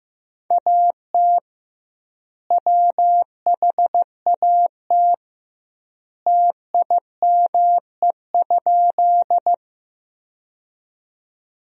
Morse code, heard as a keyed tone, about 15 wpm, 700 Hz. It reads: AT WHAT TIME?
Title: 2x Word-Spacing - Sentences from Top 100 Words - 15wpm